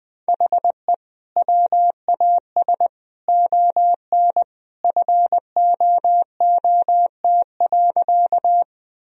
Morse code, heard as a keyed tone, about 21 wpm, 700 Hz. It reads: HE WAS ON FOOT.
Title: Sentences from Top 1000 Words - Review Part 3 - 20wpm